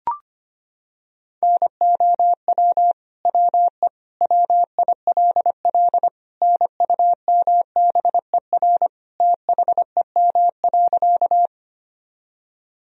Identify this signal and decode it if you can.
Morse code, heard as a keyed tone, about 25 wpm, 700 Hz. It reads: NOW WE WILL NUMBER THEM.
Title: Sentences from Top 1000 Words - Review Part 1 - 25wpm